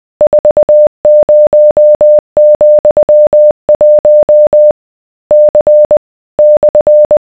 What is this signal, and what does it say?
Morse code, heard as a keyed tone, about 20 wpm, 600 Hz. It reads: 40,1 C /